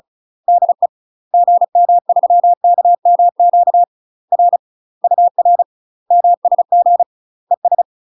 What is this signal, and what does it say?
Morse code, heard as a keyed tone, about 35 wpm, 700 Hz. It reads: DE GM3KMQ R UR MSG ES
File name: UK MASTER QSOs TRAINING / 35WPM / UK MASTER QSOs 0031